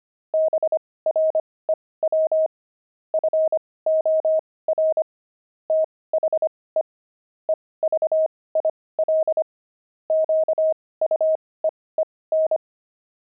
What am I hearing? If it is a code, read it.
Morse code, heard as a keyed tone, about 25 wpm, spaced as for 18 wpm, 630 Hz. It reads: BREW FOR THE EVIL QUEEN
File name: pangram-fs-18-cs-25-grumpy